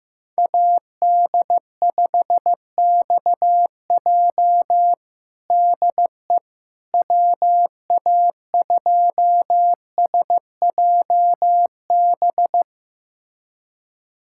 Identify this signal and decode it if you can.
Morse code, heard as a keyed tone, about 15 wpm, 700 Hz. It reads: AD5XJ DE WA2SJB
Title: USA Callsign DE Callsign 15wpm